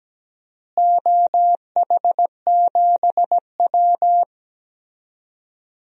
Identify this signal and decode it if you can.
Morse code, heard as a keyed tone, about 17 wpm, 700 Hz. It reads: OH7W
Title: DXCC-100 Call Signs - 4 Characters - Repeated 2x - 17wpm